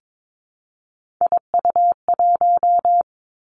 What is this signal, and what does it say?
Morse code, heard as a keyed tone, about 22 wpm, 700 Hz. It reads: IU1
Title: Three Challenging Letter-Number Combinations 22wpm